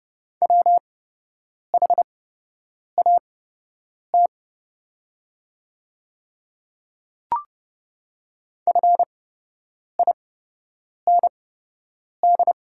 Morse code, heard as a keyed tone, about 30 wpm, spaced as for 8 wpm, 700 Hz. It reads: WHAT FIND
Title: 8x Character Spacing - Top 100 Words - 30wpm